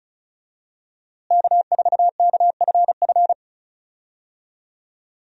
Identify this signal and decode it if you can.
Morse code, heard as a keyed tone, about 36 wpm, 700 Hz. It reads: K4KFF
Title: USA Call Signs - 5 Characters - Repeated 3x - 35wpm